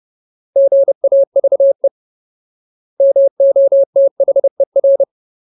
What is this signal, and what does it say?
Morse code, heard as a keyed tone, about 30 wpm, 550 Hz. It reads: GAVE MOTHER
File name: Ultimate Word List / 30WPM Ultimate